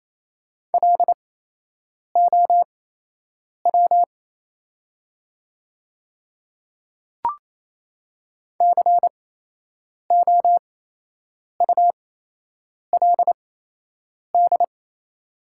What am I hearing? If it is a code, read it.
Morse code, heard as a keyed tone, about 28 wpm, spaced as for 8 wpm, 690 Hz. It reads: LOW COULD
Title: 8x Character Spacing - Top 200 Words - 28wpm